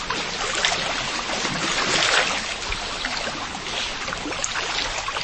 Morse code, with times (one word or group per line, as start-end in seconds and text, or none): none